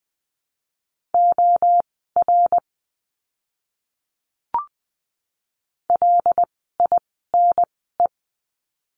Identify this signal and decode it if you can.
Morse code, heard as a keyed tone, about 20 wpm, spaced as for 15 wpm, 700 Hz. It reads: OR LINE
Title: 115 Lesson 19f - Letter w - Words 20wpm